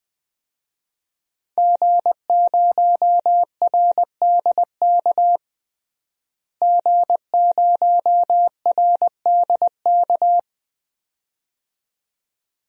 Morse code, heard as a keyed tone, about 20 wpm, 700 Hz. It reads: G0RDK G0RDK
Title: DXCC-100 Call Signs - 5 Characters - Repeated 2x - 20wpm